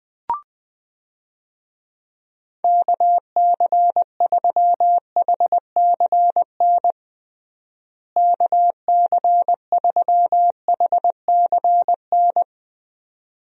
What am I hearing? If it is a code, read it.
Morse code, heard as a keyed tone, about 21 wpm, 700 Hz. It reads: KC3HCN KC3HCN
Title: USA Call Signs - 6 Characters - Repeated 2x - 20wpm